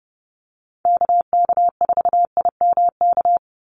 Morse code, heard as a keyed tone, about 30 wpm, 700 Hz. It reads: KK4IMK
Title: USA Call Signs 30wpm